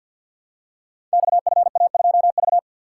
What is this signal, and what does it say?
Morse code, heard as a keyed tone, about 53 wpm, 700 Hz. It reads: XFA2V
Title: Code Groups - 5 Letter:Numbers - 50wpm